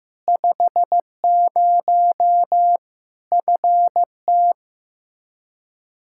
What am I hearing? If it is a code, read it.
Morse code, heard as a keyed tone, about 16 wpm, 700 Hz. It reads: 50 FT
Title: Practice QSO - Exchange 2 - Info - 15wpm